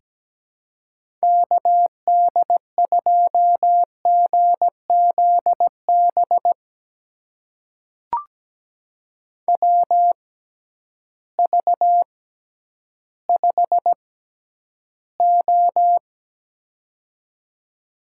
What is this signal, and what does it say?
Morse code, heard as a keyed tone, about 17 wpm, spaced as for 10 wpm, 700 Hz. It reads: KD2GZB WV5O
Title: USA Call Signs - 6x Character Spacing - 17wpm